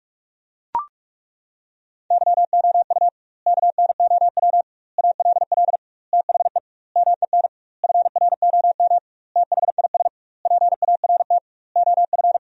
Morse code, heard as a keyed tone, about 41 wpm, 700 Hz. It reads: YOU KNOW ALL THE MEN FROM THIS PART OF
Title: 2x Word-Spacing - Sentences from Top 300 Words - 45wpm